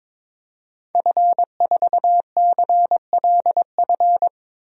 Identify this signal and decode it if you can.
Morse code, heard as a keyed tone, about 21 wpm, 700 Hz. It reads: F4CLF
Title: DXCC-100 Call Signs - 5 Characters - 22wpm